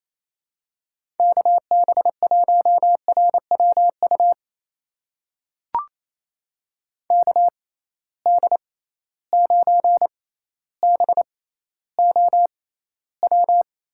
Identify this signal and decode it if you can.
Morse code, heard as a keyed tone, about 28 wpm, spaced as for 14 wpm, 700 Hz. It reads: KB1RWU KD9BOW
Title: USA Call Signs - 6 Characters - 6x Character Spacing - 28wpm